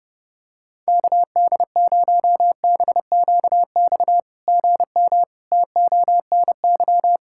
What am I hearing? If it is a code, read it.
Morse code, heard as a keyed tone, about 30 wpm, 690 Hz. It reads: KD0BQX GM TONY
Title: SOTA QSO Practice 30wpm